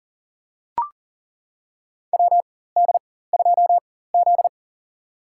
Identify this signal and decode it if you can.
Morse code, heard as a keyed tone, about 40 wpm, spaced as for 19 wpm, 700 Hz. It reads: WD2Z
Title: USA Call Signs - 4 Characters - 4x Character Spacing - 40wpm